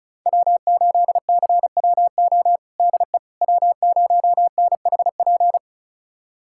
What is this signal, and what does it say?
Morse code, heard as a keyed tone, about 36 wpm, 690 Hz. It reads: W8CWO DE W0NHP
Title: USA Callsign DE Callsign - 5 Characters - 35wpm